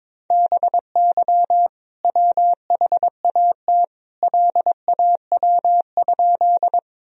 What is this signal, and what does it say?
Morse code, heard as a keyed tone, about 22 wpm, 700 Hz. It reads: BY WHAT LAW?
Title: Sentences from Top 1000 Words - Review Part 3 - 22wpm